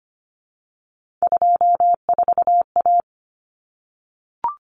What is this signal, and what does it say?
Morse code, heard as a keyed tone, about 25 wpm, 700 Hz. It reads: 24A
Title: Common Cut Numbers - 3-Digit 25wpm